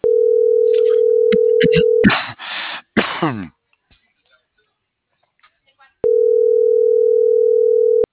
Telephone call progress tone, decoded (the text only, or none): ringback tone